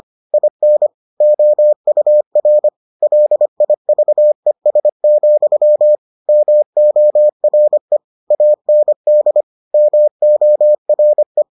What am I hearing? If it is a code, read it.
Morse code, heard as a keyed tone, about 25 wpm, 600 Hz. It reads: IN OUR LIVES, MORE AND MORE